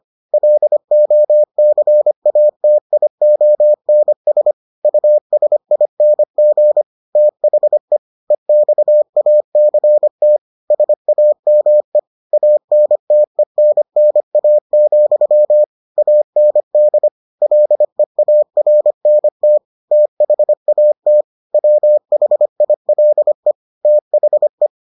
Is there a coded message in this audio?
Yes: Morse code from 0.3 to 24.7 s, heard as a keyed tone, about 25 wpm, 600 Hz. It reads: LOCATIONS USING THE EXACT SAME ANTENNA, AND LEARNT THAT WHILE THE